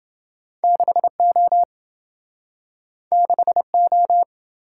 Morse code, heard as a keyed tone, about 28 wpm, 700 Hz. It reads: 6O 6O